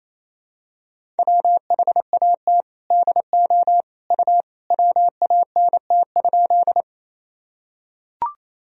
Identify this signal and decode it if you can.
Morse code, heard as a keyed tone, about 28 wpm, 700 Hz. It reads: WHAT DO U WANT?
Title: Sentences from Top 200 Words and Abbreviations 28wpm